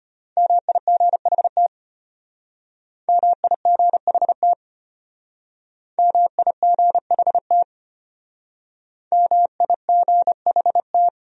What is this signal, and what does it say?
Morse code, heard as a keyed tone, about 34 wpm, 700 Hz. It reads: MIGHT MIGHT MIGHT MIGHT